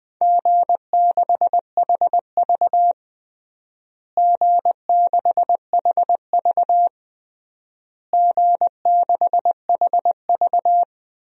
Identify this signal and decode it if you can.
Morse code, heard as a keyed tone, about 20 wpm, 700 Hz. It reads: G6HV G6HV G6HV
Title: DXCC-100 Call Signs - 4 Characters - Repeated 3x - 20wpm